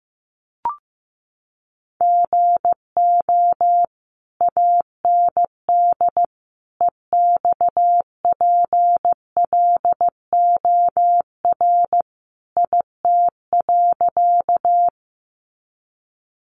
Morse code, heard as a keyed tone, about 15 wpm, 700 Hz. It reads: GO AND EXPLOR IT.
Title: Sentences from Top 1100 Words 15wpm